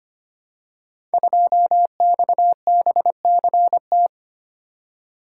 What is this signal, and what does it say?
Morse code, heard as a keyed tone, about 25 wpm, 700 Hz. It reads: 2XBCT
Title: Code Groups - 5 Letter:Numbers - 25wpm